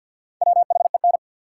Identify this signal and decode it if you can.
Morse code, heard as a keyed tone, about 49 wpm, 700 Hz. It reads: WHEN
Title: Sets of 2 Words - New Word Drill - 50wpm